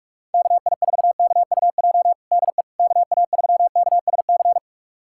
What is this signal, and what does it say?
Morse code, heard as a keyed tone, about 42 wpm, 700 Hz. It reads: KI4KUJ DE KA3KSC
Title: USA Callsign DE Callsign 45wpm